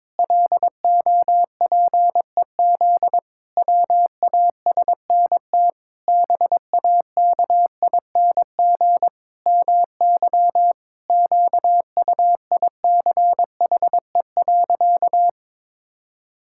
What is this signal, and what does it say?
Morse code, heard as a keyed tone, about 22 wpm, 700 Hz. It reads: LOPEZ WASNT BAKING MY QUICHE.
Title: Pangram Challenge v2 - 22wpm